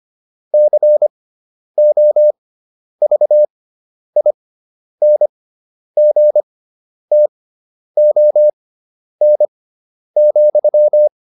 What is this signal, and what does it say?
Morse code, heard as a keyed tone, about 25 wpm, spaced as for 10 wpm, 600 Hz. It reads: COVINGTON,